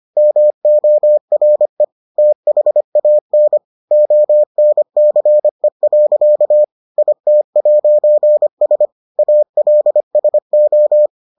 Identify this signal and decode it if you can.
Morse code, heard as a keyed tone, about 25 wpm, 600 Hz. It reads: MORE THAN ONCE. IT'S ALSO